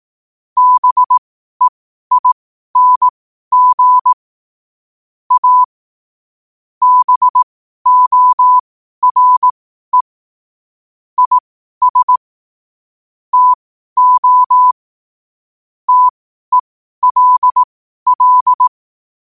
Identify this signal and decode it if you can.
Morse code, heard as a keyed tone, about 18 wpm, spaced as for 12 wpm, 1000 Hz. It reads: BEING A BORE IS TO TELL
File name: qotdfiles / 12wpm